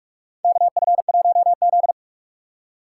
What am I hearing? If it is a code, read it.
Morse code, heard as a keyed tone, about 41 wpm, 700 Hz. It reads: KF1Z
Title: USA Call Signs - 2x Character Spacing - 45wpm